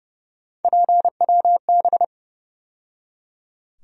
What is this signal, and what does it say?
Morse code, heard as a keyed tone, about 30 wpm, 700 Hz. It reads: PWB
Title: USA Call Sign Postfixes - 3 Letters 30wpm